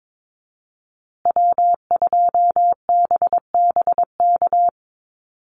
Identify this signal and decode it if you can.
Morse code, heard as a keyed tone, about 22 wpm, 700 Hz. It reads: W2BBK
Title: USA Call Signs Repeated 2x 22wpm